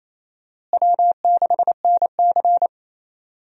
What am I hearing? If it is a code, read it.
Morse code, heard as a keyed tone, about 28 wpm, 700 Hz. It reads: W6NC
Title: USA Call Signs - 4 Characters - Repeated 3x - 28wpm